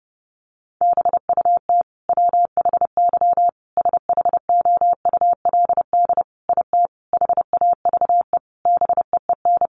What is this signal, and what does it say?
Morse code, heard as a keyed tone, about 28 wpm, 700 Hz. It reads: BUT WHY SHOULD IT HAVE BEEN
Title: Sentences from Top 1200 Words 30wpm